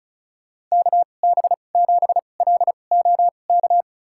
Morse code, heard as a keyed tone, about 34 wpm, spaced as for 25 wpm, 700 Hz. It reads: KB7LOK
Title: USA Call Signs - 6 Characters - 2x Character Spacing - 35wpm